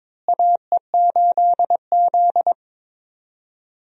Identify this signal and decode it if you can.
Morse code, heard as a keyed tone, about 22 wpm, 700 Hz. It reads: AE8Z
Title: USA Call Signs - 4 Characters - Repeated 3x - 22wpm